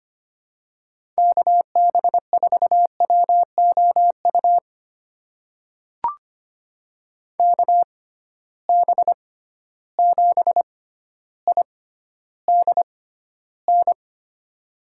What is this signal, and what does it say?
Morse code, heard as a keyed tone, about 25 wpm, spaced as for 13 wpm, 700 Hz. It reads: KB4WOU KB7IDN
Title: USA Call Signs - 6 Characters - 6x Character Spacing - 25wpm